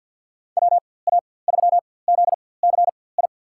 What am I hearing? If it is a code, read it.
Morse code, heard as a keyed tone, about 47 wpm, spaced as for 23 wpm, 700 Hz. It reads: WA3ZCI
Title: USA Call Signs - 4x Character Spacing - 50wpm